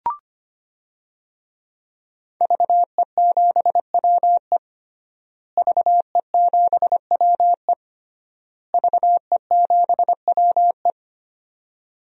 Morse code, heard as a keyed tone, about 25 wpm, 700 Hz. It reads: VE7WE VE7WE VE7WE
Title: DXCC-100 Call Signs - 5 Characters - Repeated 3x - 25wpm